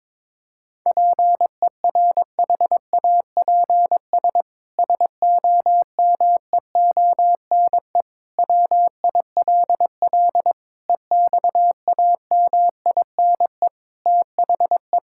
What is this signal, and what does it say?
Morse code, heard as a keyed tone, about 22 wpm, 700 Hz. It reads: PERHAPS SOMEONE WILL EXAMINE THE
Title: Sentences from Top 1600 Words 22wpm